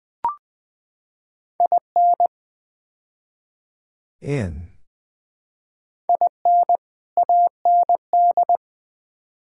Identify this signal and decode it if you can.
Morse code, heard as a keyed tone, about 20 wpm, 700 Hz. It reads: IN IN AND
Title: Build A Phrase - 5-Word Phrases - 20wpm